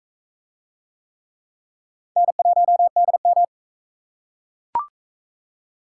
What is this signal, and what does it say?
Morse code, heard as a keyed tone, about 44 wpm, 700 Hz. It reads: N1DM